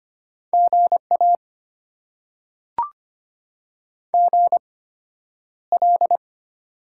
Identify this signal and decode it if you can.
Morse code, heard as a keyed tone, about 25 wpm, spaced as for 11 wpm, 700 Hz. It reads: GA GL
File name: Top 90 QSO Elements - 8x Character Spacing - 25wpm